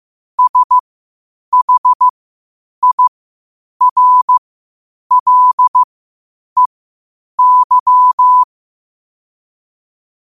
Morse code, heard as a keyed tone, about 15 wpm, spaced as for 8 wpm, 1000 Hz. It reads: SHIRLEY
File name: qotdfiles / 8wpm